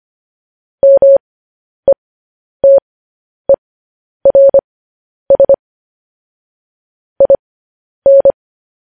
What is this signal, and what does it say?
Morse code, heard as a keyed tone, about 25 wpm, spaced as for 10 wpm, 550 Hz. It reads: METERS IN
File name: podcast / News headlines-1x-25x10